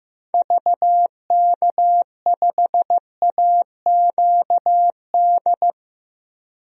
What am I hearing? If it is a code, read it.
Morse code, heard as a keyed tone, about 15 wpm, 700 Hz. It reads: VK5AQD
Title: DXCC-100 Call Signs - 6 Characters - 15wpm